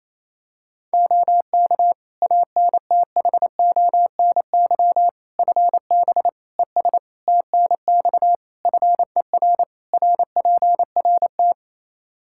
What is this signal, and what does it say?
Morse code, heard as a keyed tone, about 28 wpm, 700 Hz. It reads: OK ANTHONY FB ES TNX FER RPRT
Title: Practice QSO - Exchange 2 - Info - 28wpm